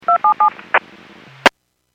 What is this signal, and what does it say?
DTMF, heard as telephone touch tones, tones of about 110 ms. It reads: 277
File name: WPS / 00346 14-43 WPS 451.25 WPS